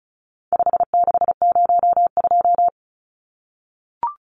Morse code, heard as a keyed tone, about 34 wpm, 700 Hz. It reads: 5602
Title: Numbers - 4-Digit 35wpm